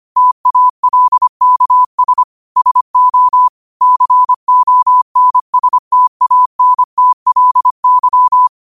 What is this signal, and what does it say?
Morse code, heard as a keyed tone, about 25 wpm, 1000 Hz. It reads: TALKS SO CONSTANTLY
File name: qotdfiles / 25wpm